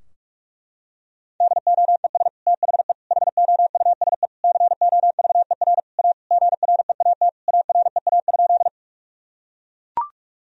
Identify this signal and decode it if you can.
Morse code, heard as a keyed tone, about 42 wpm, 700 Hz. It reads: DOES THE HOUSE COVER A GREAT AREA?
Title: Sentences from Top 1000 Words - Review Part 2 - 45wpm